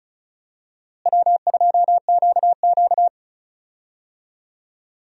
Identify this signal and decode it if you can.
Morse code, heard as a keyed tone, about 36 wpm, 700 Hz. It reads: W2QQ
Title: USA Call Signs - 4 Characters - Repeated 3x - 35wpm